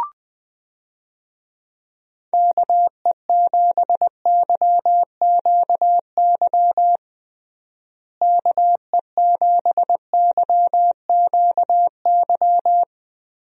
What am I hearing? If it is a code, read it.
Morse code, heard as a keyed tone, about 20 wpm, 700 Hz. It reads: KE7YQY KE7YQY
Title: USA Call Signs - 6 Characters - Repeated 3x - 20wpm